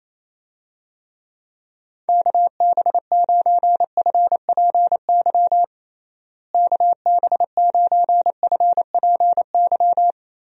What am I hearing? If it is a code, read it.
Morse code, heard as a keyed tone, about 28 wpm, 700 Hz. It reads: KB9FPY KB9FPY
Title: USA Call Signs - 6 Characters - Repeated 3x - 28wpm